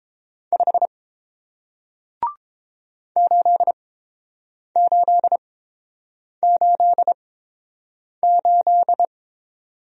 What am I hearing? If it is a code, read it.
Morse code, heard as a keyed tone, about 30 wpm, spaced as for 7 wpm, 700 Hz. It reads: 5 8888